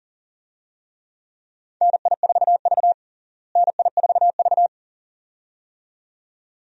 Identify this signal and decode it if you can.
Morse code, heard as a keyed tone, about 40 wpm, 700 Hz. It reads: NI4V NI4V
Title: USA Call Signs - 4 Characters - Repeated 2x - 40wpm